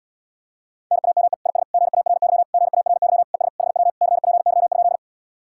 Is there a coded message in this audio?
Yes: Morse code from 0.9 to 5.0 s, heard as a keyed tone, about 80 wpm, 700 Hz. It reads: NAME IS GARY GARY ES HW CPY?